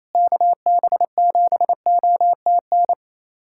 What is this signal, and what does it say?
Morse code, heard as a keyed tone, about 27 wpm, 700 Hz. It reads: KB7OTN
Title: USA Callsign DE Callsign 28wpm